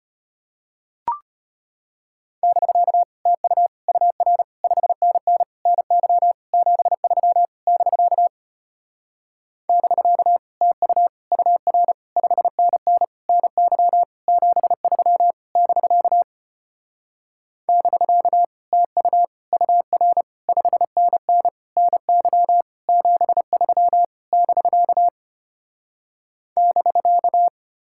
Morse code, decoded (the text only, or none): <BK> TU UR 5NN NY 73 <BK> <BK> TU UR 5NN NY 73 <BK> <BK> TU UR 5NN NY 73 <BK> <BK>